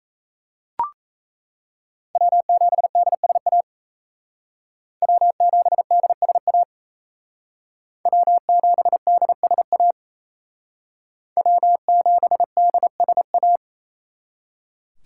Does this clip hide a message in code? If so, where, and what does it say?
Yes: Morse code from 2.1 to 13.6 s, heard as a keyed tone, about 36 wpm, 700 Hz. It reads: W7DSA W7DSA W7DSA W7DSA